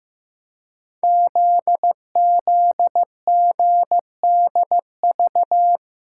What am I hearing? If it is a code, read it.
Morse code, heard as a keyed tone, about 15 wpm, 700 Hz. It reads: ZZGDV
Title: Code Groups - 5 Letters - 15wpm